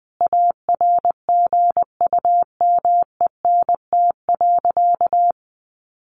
Morse code, heard as a keyed tone, about 20 wpm, 700 Hz. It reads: ARGUMENT.
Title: Sentences from Top 1400 Words 20wpm